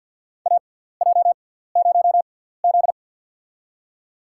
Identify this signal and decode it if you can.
Morse code, heard as a keyed tone, about 52 wpm, spaced as for 17 wpm, 700 Hz. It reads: AJ0Z
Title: USA Call Signs - 4 Characters - 6x Character Spacing - 50wpm